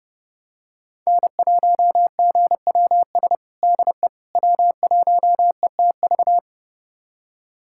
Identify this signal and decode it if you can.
Morse code, heard as a keyed tone, about 32 wpm, 700 Hz. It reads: N1GWS DE W1ETV
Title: USA Callsign DE Callsign - 5 Characters - 30wpm